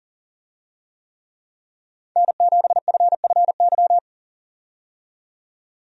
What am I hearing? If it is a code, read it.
Morse code, heard as a keyed tone, about 37 wpm, 700 Hz. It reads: N7FFY
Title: USA Call Signs - 5 Characters - 40wpm